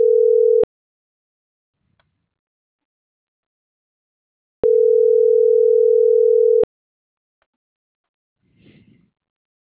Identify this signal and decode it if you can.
Ringback tone (North America), 440 + 480 Hz.